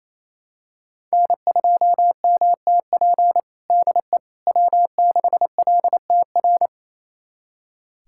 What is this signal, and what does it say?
Morse code, heard as a keyed tone, about 28 wpm, 700 Hz. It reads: N2MTP DE W6LTR
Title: USA Callsign DE Callsign - 5 Characters - 28wpm